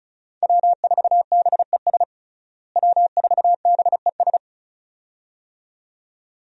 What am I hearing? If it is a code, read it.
Morse code, heard as a keyed tone, about 36 wpm, 700 Hz. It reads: W4BES W4BES
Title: USA Call Signs - 5 Characters - Repeated 2x - 35wpm